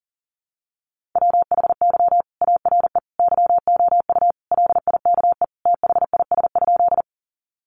Morse code, heard as a keyed tone, about 37 wpm, 700 Hz. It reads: WHY ARE YOU LIKE THIS?
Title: Sentences from Top 200 Words 40wpm